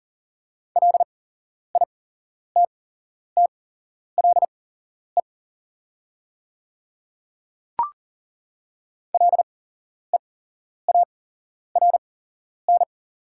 Morse code, heard as a keyed tone, about 40 wpm, spaced as for 11 wpm, 700 Hz. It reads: LITTLE LEARN
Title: 8x Character Spacing - Top 300 Words - 40wpm